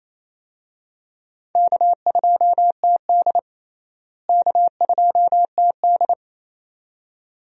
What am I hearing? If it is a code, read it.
Morse code, heard as a keyed tone, about 28 wpm, 700 Hz. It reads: K2TD K2TD
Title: USA Call Signs - 4 Characters - Repeated 2x - 28wpm